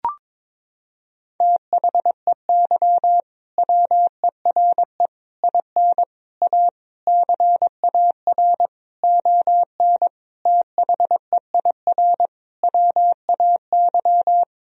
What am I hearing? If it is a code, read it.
Morse code, heard as a keyed tone, about 22 wpm, 700 Hz. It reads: THEY WERE IN A CAR ON THEIR WAY
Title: Sentences from Top 1000 Words - Review Part 1 - 22wpm